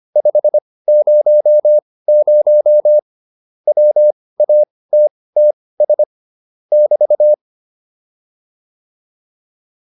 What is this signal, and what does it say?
Morse code, heard as a keyed tone, about 25 wpm, spaced as for 18 wpm, 600 Hz. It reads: 500 WATTS <BT>
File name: QSO408 18